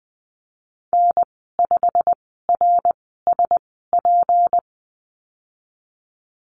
Morse code, heard as a keyed tone, about 20 wpm, spaced as for 15 wpm, 700 Hz. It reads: N5RSP
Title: 154 Lesson 25g - Letter p - Call Signs 20wpm